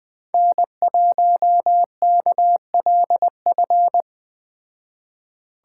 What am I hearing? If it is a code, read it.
Morse code, heard as a keyed tone, about 20 wpm, 700 Hz. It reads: N1KLF
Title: Speed Racing - USA 1x3 Call Signs 17wpm